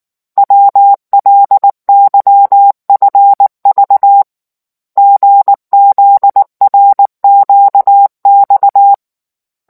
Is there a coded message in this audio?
Yes: Morse code from 0.4 to 8.9 s, heard as a keyed tone, about 19 wpm, 800 Hz. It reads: WLYFV GZRQX